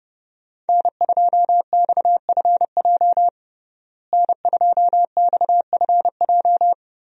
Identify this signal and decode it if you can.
Morse code, heard as a keyed tone, about 30 wpm, 700 Hz. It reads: N2XFJ N2XFJ
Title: USA Call Signs - 5 Characters - Repeated 3x - 30wpm